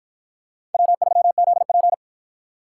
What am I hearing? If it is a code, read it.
Morse code, heard as a keyed tone, about 50 wpm, 700 Hz. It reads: W3ZP